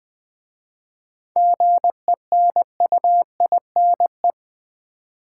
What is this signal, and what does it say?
Morse code, heard as a keyed tone, about 20 wpm, 700 Hz. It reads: GENUINE
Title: 7 Letter Words - First 3 Letters Given - 20wpm